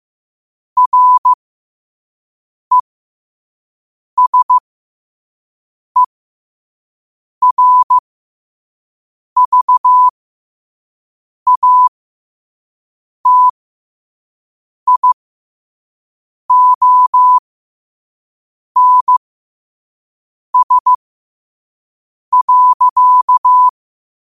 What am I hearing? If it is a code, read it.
Morse code, heard as a keyed tone, about 15 wpm, spaced as for 5 wpm, 1000 Hz. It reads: RESERVATIONS.
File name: qotdfiles / 5wpm